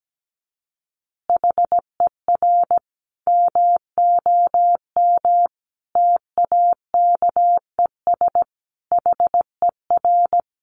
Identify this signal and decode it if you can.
Morse code, heard as a keyed tone, about 17 wpm, 700 Hz. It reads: HER MOM TAKES HER
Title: Additional Easy Phrases 17wpm